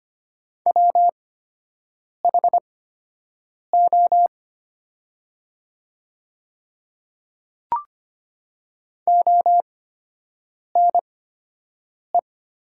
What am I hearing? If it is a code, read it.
Morse code, heard as a keyed tone, about 25 wpm, spaced as for 7 wpm, 700 Hz. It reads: WHO ONE
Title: 8x Character Spacing - Top 100 Words - 25wpm